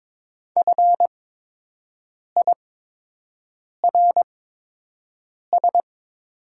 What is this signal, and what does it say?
Morse code, heard as a keyed tone, about 22 wpm, spaced as for 6 wpm, 700 Hz. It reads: FIRS
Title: 8x Character Spacing - Top 100 Words - 22wpm